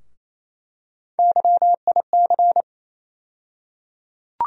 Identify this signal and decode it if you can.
Morse code, heard as a keyed tone, about 28 wpm, 700 Hz. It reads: YIC
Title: USA Call Sign Postfixes - 3 Letters 28wpm